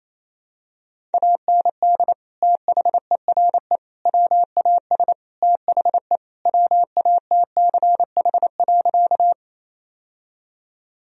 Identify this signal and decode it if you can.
Morse code, heard as a keyed tone, about 28 wpm, 700 Hz. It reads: AND THERE WAS THE WATCH.
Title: Sentences from Top 1000 Words - Review Part 1 - 28wpm